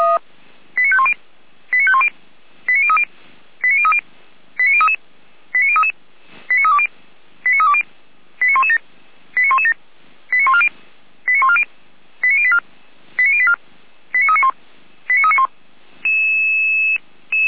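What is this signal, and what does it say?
Five-tone selective call (ZVEI), heard as quick runs of five tones, tones of about 70 ms. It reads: 89510 89510 89030 89030 89033 89033 89320 89320 89117 89117 89140 89140 89084 89084 89391 89391
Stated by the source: ZVEI five-tone selective call 89099, received by SDR, from feuerwehr-orlamuende (fire brigade).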